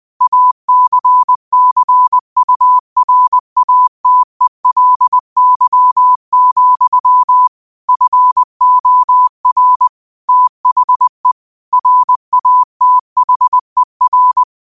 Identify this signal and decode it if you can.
Morse code, heard as a keyed tone, about 20 wpm, 1000 Hz. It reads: ACCURATELY, FOR THE RATHER